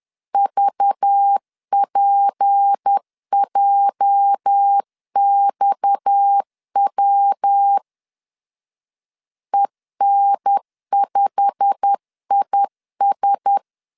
Morse code, heard as a keyed tone, about 10 wpm, 800 Hz. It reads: VPJXW EN5IS